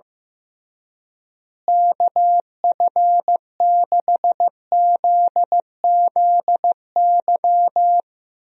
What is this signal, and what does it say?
Morse code, heard as a keyed tone, about 15 wpm, 700 Hz. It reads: KF6ZZY
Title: USA Call Signs - 6 Characters - Repeated 3x - 15wpm